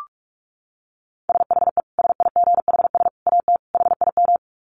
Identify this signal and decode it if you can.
Morse code, heard as a keyed tone, about 44 wpm, 700 Hz. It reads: SHE SIGHS AT HIM
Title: Dit Challenge 45wpm